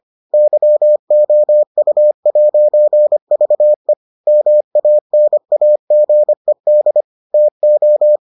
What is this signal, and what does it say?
Morse code, heard as a keyed tone, about 25 wpm, 600 Hz. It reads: YOU'VE MANAGED TO